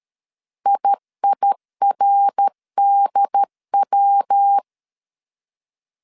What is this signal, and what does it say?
Morse code, heard as a keyed tone, about 13 wpm, 800 Hz. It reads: IIRDW
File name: CW / DET / 1 6 1500C S A 12wpm 3440